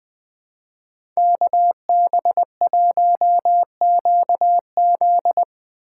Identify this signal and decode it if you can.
Morse code, heard as a keyed tone, about 23 wpm, 700 Hz. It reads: KB1QZ
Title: Speed Racing - USA Call Signs 20wpm